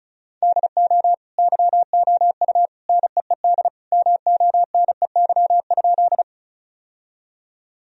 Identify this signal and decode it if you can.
Morse code, heard as a keyed tone, about 34 wpm, 700 Hz. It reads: DO YOU NEED MONEY?
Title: Sentences from Top 1000 Words - Review Part 1 - 35wpm